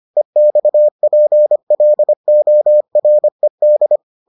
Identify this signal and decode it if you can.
Morse code, heard as a keyed tone, about 25 wpm, 600 Hz. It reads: EXPLORED